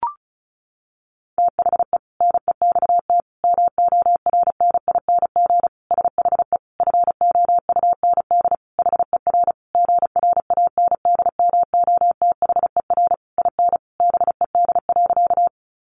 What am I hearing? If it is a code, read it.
Morse code, heard as a keyed tone, about 34 wpm, 700 Hz. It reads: THE NEXT MORNING SHE FOUND HER GRANDMOTHER IN BED.
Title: Sentences from Top 1200 Words 35wpm